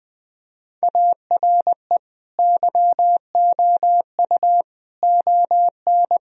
Morse code, heard as a keyed tone, about 21 wpm, 700 Hz. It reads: ARE YOU ON
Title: Sentences from Top 1000 Words - Review Part 2 - 20wpm